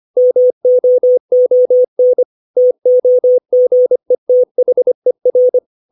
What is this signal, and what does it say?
Morse code, heard as a keyed tone, about 25 wpm, 500 Hz. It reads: MOON TOGETHER
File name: 632 words / 632 Words 25WPM